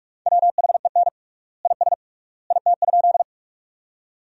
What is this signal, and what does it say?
Morse code, heard as a keyed tone, about 41 wpm, 700 Hz. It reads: WHEN IS IT?
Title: 3x Word-Spacing - Sentences from Top 100 Words - 45wpm